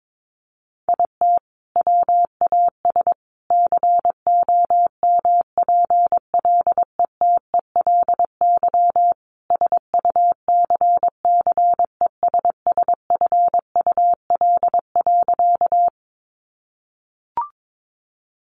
Morse code, heard as a keyed tone, about 22 wpm, 700 Hz. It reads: IT WAS COMPLETELY SUCCESSFUL.
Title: Sentences from Top 1100 Words 22wpm